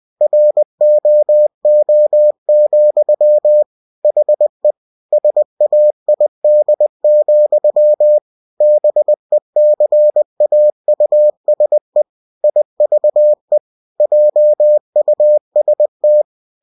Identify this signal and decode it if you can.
Morse code, heard as a keyed tone, about 20 wpm, 600 Hz. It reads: ROO, HE SAID, BECAUSE IVE JUST